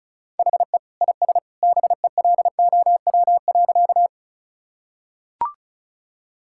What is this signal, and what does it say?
Morse code, heard as a keyed tone, about 36 wpm, 700 Hz. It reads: HE IS BELOW.
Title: Sentences from Top 1000 Words - Review Part 5 - 35wpm